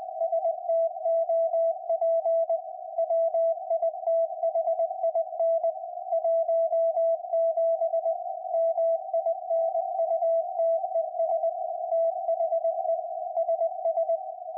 Morse code, heard as a keyed tone, about 20 wpm, 660 Hz, with radio noise. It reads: STOP WITHIN 17 MINUTES THE SS
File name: IC-7100